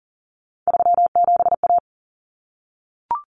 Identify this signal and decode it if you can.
Morse code, heard as a keyed tone, about 40 wpm, 700 Hz. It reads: 37A
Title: Common Cut Numbers - 3-Digit 40wpm